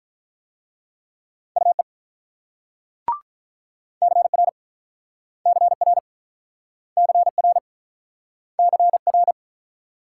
Morse code, heard as a keyed tone, about 44 wpm, 700 Hz. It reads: UE CR CR CR CR
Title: Speed-Racing Multi-letter Phonemes - 35wpm